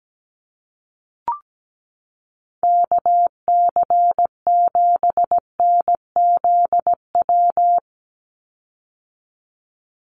Morse code, heard as a keyed tone, about 17 wpm, 700 Hz. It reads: KC7NZW
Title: USA Call Signs 17wpm